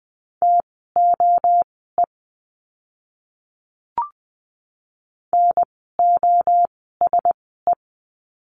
Morse code, heard as a keyed tone, about 20 wpm, spaced as for 15 wpm, 700 Hz. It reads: TOE NOSE
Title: 036 Lesson 7f - Letter s - Words 20wpm